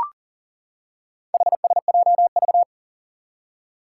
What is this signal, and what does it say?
Morse code, heard as a keyed tone, about 39 wpm, 700 Hz. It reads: HSJV
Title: Four Challenging Letter-Number Combinations 40wpm